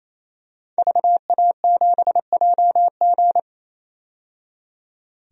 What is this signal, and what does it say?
Morse code, heard as a keyed tone, about 28 wpm, 700 Hz. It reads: VA7JG
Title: DXCC-100 Call Signs - 5 Characters - 28wpm